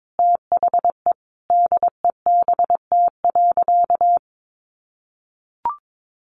Morse code, heard as a keyed tone, about 22 wpm, 700 Hz. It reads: THE DEBT.
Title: Sentences from Top 1300 Words 22wpm